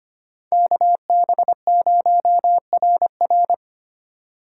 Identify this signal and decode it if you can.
Morse code, heard as a keyed tone, about 25 wpm, 700 Hz. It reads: KB0RR
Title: USA Call Signs - 5 Characters - 2x Character Spacing - 25wpm